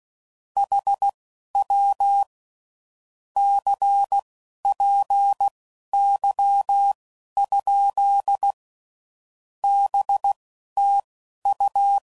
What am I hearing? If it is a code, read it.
Morse code, heard as a keyed tone, about 16 wpm, spaced as for 11 wpm, 800 Hz. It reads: HW CPY? BTU
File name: CW Code Groups @10WPM / QSOs / QSO WA7QOX